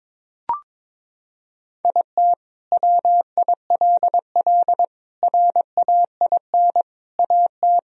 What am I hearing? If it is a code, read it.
Morse code, heard as a keyed tone, about 22 wpm, 700 Hz. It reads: IT WILL RAIN AT
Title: Sentences from Top 1000 Words - Review Part 1 - 22wpm